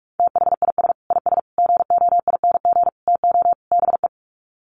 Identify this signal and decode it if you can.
Morse code, heard as a keyed tone, about 44 wpm, 700 Hz. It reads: THIS IS GOING TO BE
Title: Sets of 5 Words 45wpm